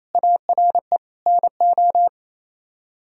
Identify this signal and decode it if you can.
Morse code, heard as a keyed tone, about 35 wpm, 700 Hz. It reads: ARE NO
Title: Speed-Racing - Sets of 3 Words 28wpm